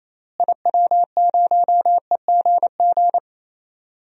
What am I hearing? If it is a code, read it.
Morse code, heard as a keyed tone, about 28 wpm, 700 Hz. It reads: IW0EGG
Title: DXCC-100 Call Signs - 6 Characters - Repeated 2x - 28wpm